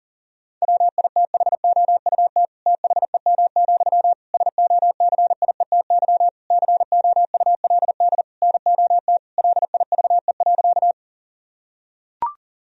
Morse code, heard as a keyed tone, about 40 wpm, 700 Hz. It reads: WITHOUT THEM, SOCIETY COULD NOT LIVE.
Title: Sentences from Top 1000 Words - Review Part 4 - 40wpm